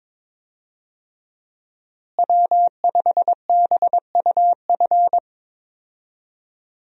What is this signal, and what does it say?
Morse code, heard as a keyed tone, about 22 wpm, 700 Hz. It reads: W5BUF
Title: USA Call Signs - 5 Characters - Repeated 3x - 22wpm